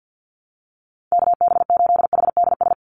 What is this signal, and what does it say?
Morse code, heard as a keyed tone, about 48 wpm, 700 Hz. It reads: KB8HDS
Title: USA Call Signs 50wpm